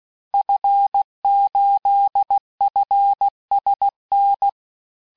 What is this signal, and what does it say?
Morse code, heard as a keyed tone, about 16 wpm, 800 Hz. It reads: F8FSN
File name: cours-du-lundi files / cours-1580-fin-QSO